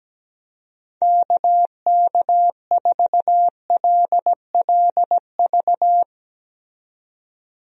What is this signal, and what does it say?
Morse code, heard as a keyed tone, about 17 wpm, 700 Hz. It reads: KK4LLV
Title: USA Call Signs - 6 Characters - 17wpm